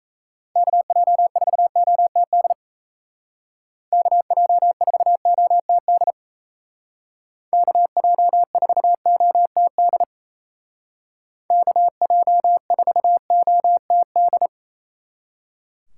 Morse code, heard as a keyed tone, about 35 wpm, 700 Hz. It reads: KJ4OTD KJ4OTD KJ4OTD KJ4OTD